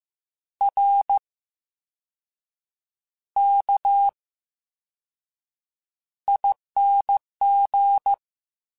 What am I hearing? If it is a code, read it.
Morse code, heard as a keyed tone, about 15 wpm, spaced as for 6 wpm, 800 Hz. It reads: RKING
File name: Special INGs-mp3s / Special INGs-mp3s / VerySlowToFaster Ing @13